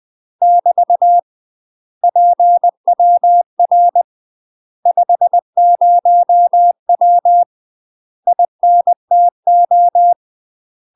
Morse code, heard as a keyed tone, about 20 wpm, 700 Hz. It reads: <BT> PWR 50W INTO
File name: UK MASTER QSOs TRAINING / 20WPM / UK MASTER QSOs 0024